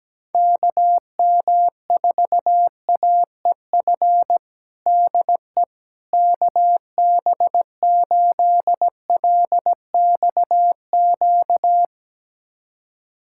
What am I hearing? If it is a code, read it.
Morse code, heard as a keyed tone, about 17 wpm, 700 Hz. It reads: KM4AEF DE KB8LXQ
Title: USA Callsign DE Callsign - 6 Characters - 17wpm